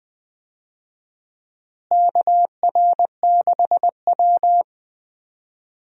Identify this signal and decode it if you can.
Morse code, heard as a keyed tone, about 20 wpm, 700 Hz. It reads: KR6W
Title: USA Call Signs - 4 Characters - Repeated 2x - 20wpm